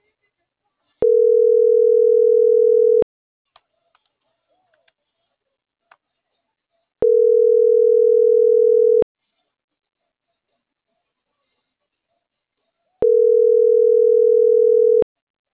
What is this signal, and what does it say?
Ringback tone (North America), 440 + 480 Hz.